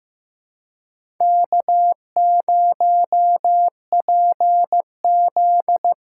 Morse code, heard as a keyed tone, about 15 wpm, 700 Hz. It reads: K0PZ
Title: USA Callsign DE Callsign - 4 Characters - 15wpm